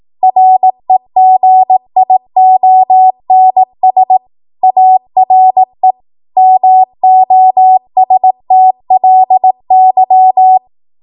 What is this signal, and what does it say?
Morse code, heard as a keyed tone, about 18 wpm, 750 Hz. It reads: REGIONS ARE MOSTLY
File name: Morse / 18WPM